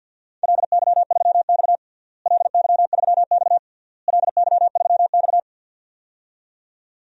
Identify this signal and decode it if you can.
Morse code, heard as a keyed tone, about 51 wpm, 700 Hz. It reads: LY3X LY3X LY3X
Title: DXCC-100 Call Signs - 4 Characters - Repeated 3x - 50wpm